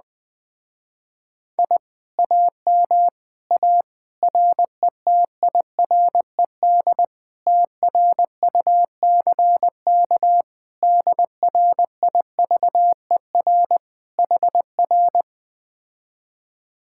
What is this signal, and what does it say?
Morse code, heard as a keyed tone, about 20 wpm, 700 Hz. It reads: I AM A RETIRED TRUCK DRIVER HR
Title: Speed-Racing Common QSO Phrases Encore - 17wpm